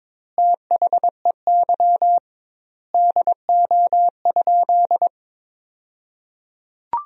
Morse code, heard as a keyed tone, about 22 wpm, 700 Hz. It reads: THEY DO?
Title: 2x Word-Spacing - Sentences from Top 100 Words - 22wpm